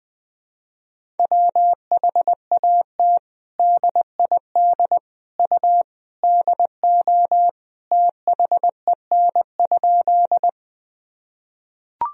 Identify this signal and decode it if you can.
Morse code, heard as a keyed tone, about 21 wpm, 700 Hz. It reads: WHAT DID U DO THEN?
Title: Sentences from Top 100 Words and Abbreviations 20wpm